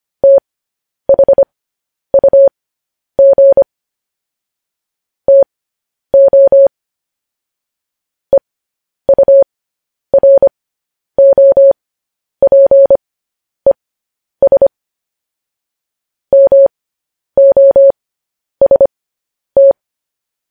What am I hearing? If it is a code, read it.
Morse code, heard as a keyed tone, about 25 wpm, spaced as for 10 wpm, 550 Hz. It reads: THUG TO EUROPES MOST